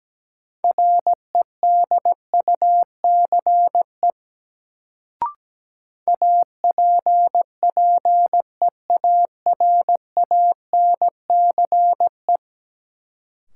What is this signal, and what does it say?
Morse code, heard as a keyed tone, about 17 wpm, 700 Hz. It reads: REDUCE APPEARANCE
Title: Sentences from Top 1100 Words - New Word Drill - 17wpm